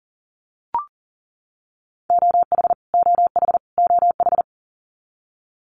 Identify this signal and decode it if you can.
Morse code, heard as a keyed tone, about 40 wpm, 700 Hz. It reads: OH OH OH
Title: Top 500 Words Repeated 40wpm